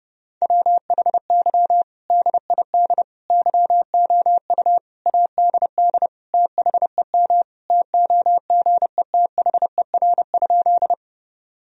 Morse code, heard as a keyed tone, about 30 wpm, 700 Hz. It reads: WHY DID YOU ADD THEM TOGETHER?
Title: Sentences from Top 1000 Words - Review Part 1 - 30wpm